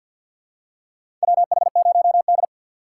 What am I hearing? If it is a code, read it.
Morse code, heard as a keyed tone, about 53 wpm, 700 Hz. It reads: WH0D
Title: USA Call Signs - 4x Character Spacing - 50wpm